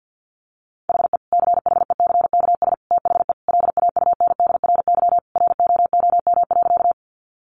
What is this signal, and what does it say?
Morse code, heard as a keyed tone, about 50 wpm, 700 Hz. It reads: HE CHECKS THE LAUNDRY ROOM.
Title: Easy Phrases 50wpm